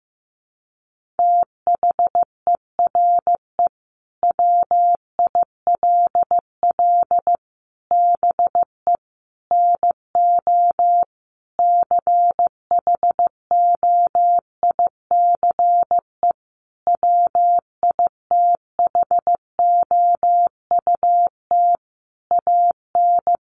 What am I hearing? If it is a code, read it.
Morse code, heard as a keyed tone, about 15 wpm, 700 Hz. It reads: THERE WILL BE NO CHOICE WITHOUT AN
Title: Sentences from Top 1500 Words 15wpm